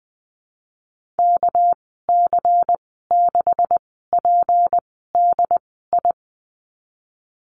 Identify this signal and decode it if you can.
Morse code, heard as a keyed tone, about 20 wpm, spaced as for 15 wpm, 700 Hz. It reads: KC6PDI
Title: 245 Lesson 39g - Letter z - Call Signs 20wpm